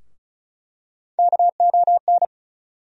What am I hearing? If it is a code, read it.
Morse code, heard as a keyed tone, about 34 wpm, 700 Hz. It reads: KON
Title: USA Call Sign Postfixes - 3 Letters 35wpm